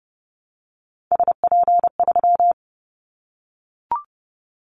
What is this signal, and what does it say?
Morse code, heard as a keyed tone, about 30 wpm, 700 Hz. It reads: SP3
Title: Callsign Prefixes for Top 25 DXCC - 30wpm